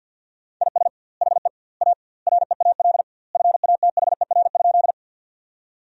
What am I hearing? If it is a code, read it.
Morse code, heard as a keyed tone, about 50 wpm, 700 Hz. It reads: IS HE A REAL FATHER?
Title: 2x Word-Spacing - Sentences from Top 300 Words - 50wpm